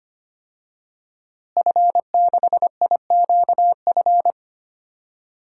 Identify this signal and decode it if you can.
Morse code, heard as a keyed tone, about 25 wpm, 700 Hz. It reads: F6IQF
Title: DXCC-100 Call Signs - 5 Characters - Repeated 2x - 25wpm